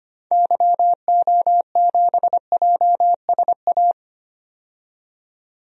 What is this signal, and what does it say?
Morse code, heard as a keyed tone, about 25 wpm, 700 Hz. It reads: YO7JSA
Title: DXCC-100 Call Signs - 6 Characters - 25wpm